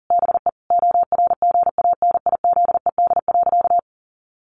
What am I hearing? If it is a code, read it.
Morse code, heard as a keyed tone, about 40 wpm, 700 Hz. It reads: BE ORGANIZED.
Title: Sentences from Top 1300 Words 40wpm